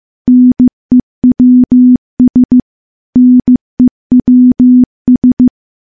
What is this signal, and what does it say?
Morse code, heard as a keyed tone, about 15 wpm, 250 Hz. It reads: NEWS NEWS
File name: morse-training-15WPM